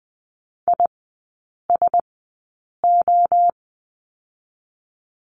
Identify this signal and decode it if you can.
Morse code, heard as a keyed tone, about 20 wpm, spaced as for 8 wpm, 700 Hz. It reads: ISO
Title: 067 Lesson 12e - Letter d - Three Characters 20wpm